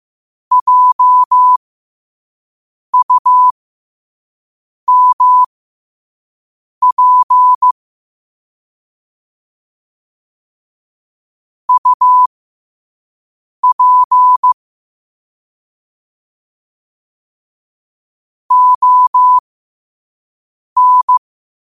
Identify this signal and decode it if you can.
Morse code, heard as a keyed tone, about 15 wpm, spaced as for 5 wpm, 1000 Hz. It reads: JUMP UP ON
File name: qotdfiles / 5wpm